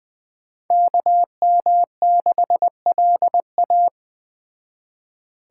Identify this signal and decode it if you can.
Morse code, heard as a keyed tone, about 20 wpm, 700 Hz. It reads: KM6LA